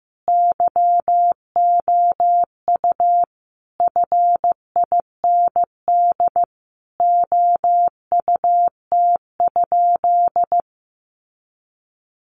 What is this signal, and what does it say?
Morse code, heard as a keyed tone, about 15 wpm, 700 Hz. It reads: YOU FIND OUT?
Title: Sentences from Top 100 Words 15wpm